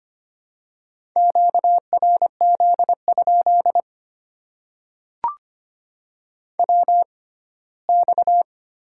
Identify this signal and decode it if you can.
Morse code, heard as a keyed tone, about 25 wpm, spaced as for 17 wpm, 700 Hz. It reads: QRZ? WX
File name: Top 90 QSO Elements - 6x Character Spacing - 25wpm